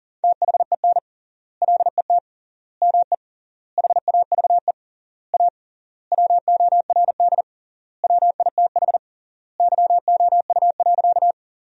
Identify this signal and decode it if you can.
Morse code, heard as a keyed tone, about 40 wpm, 700 Hz. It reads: THEN LET ME HAVE A WORD WITH YOU.
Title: 3x Word-Spacing - Sentences from Top 300 Words - 40wpm